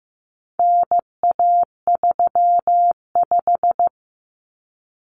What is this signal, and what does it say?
Morse code, heard as a keyed tone, about 15 wpm, 700 Hz. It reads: NA35
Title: Common Cut Numbers - 4-Digit 15wpm